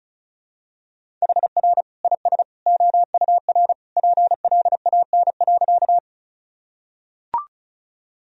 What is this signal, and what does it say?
Morse code, heard as a keyed tone, about 36 wpm, 700 Hz. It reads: HR IS OUR PLAN.
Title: Sentences from Top 500 Words and Abbreviations 35wpm